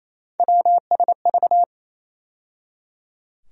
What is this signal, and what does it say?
Morse code, heard as a keyed tone, about 28 wpm, 700 Hz. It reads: WSV
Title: USA Call Sign Postfixes - 3 Letters 28wpm